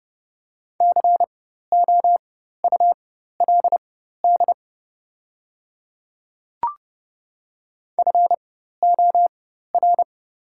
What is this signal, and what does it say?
Morse code, heard as a keyed tone, about 31 wpm, spaced as for 14 wpm, 700 Hz. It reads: COULD FOR
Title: 4x Character Spacing - Top 100 Words - 30wpm